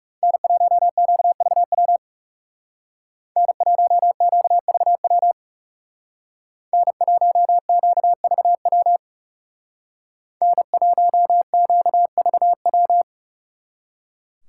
Morse code, heard as a keyed tone, about 37 wpm, 700 Hz. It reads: N1QVW N1QVW N1QVW N1QVW
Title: Speed Racing - USA 1x3 Call Signs 30wpm